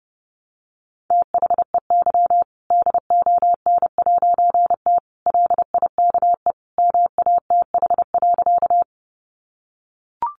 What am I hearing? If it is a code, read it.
Morse code, heard as a keyed tone, about 30 wpm, 700 Hz. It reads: THEY DON'T LIKE MATH.
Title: Easy Phrases 30wpm